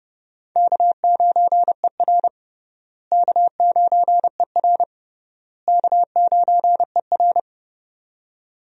Morse code, heard as a keyed tone, about 30 wpm, 700 Hz. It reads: K9ER K9ER K9ER
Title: USA Call Signs - 4 Characters - Repeated 3x - 30wpm